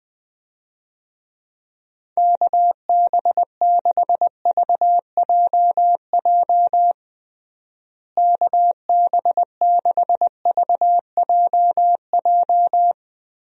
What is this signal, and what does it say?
Morse code, heard as a keyed tone, about 20 wpm, 700 Hz. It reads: KB6VJJ KB6VJJ
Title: USA Call Signs - 6 Characters - Repeated 3x - 20wpm